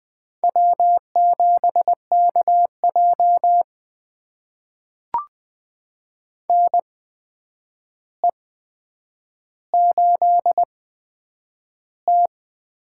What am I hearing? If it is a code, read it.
Morse code, heard as a keyed tone, about 20 wpm, spaced as for 9 wpm, 700 Hz. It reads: W7KJ NE8T
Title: USA Call Signs - 4 Characters - 8x Character Spacing - 20wpm